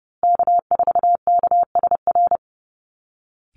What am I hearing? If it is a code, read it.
Morse code, heard as a keyed tone, about 30 wpm, 700 Hz. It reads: K4KSR
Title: Call Signs Repeated 3x - 30wpm